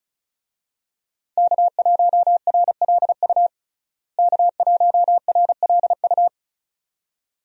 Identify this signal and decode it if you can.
Morse code, heard as a keyed tone, about 36 wpm, 700 Hz. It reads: K1RLU K1RLU
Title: USA Call Signs - 5 Characters - Repeated 2x - 35wpm